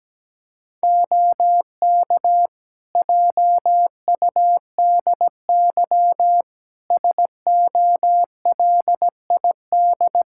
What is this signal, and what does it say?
Morse code, heard as a keyed tone, about 17 wpm, 700 Hz. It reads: OK JUDY SOLID
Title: Practice QSO - Exchange 3 - More Info - 17wpm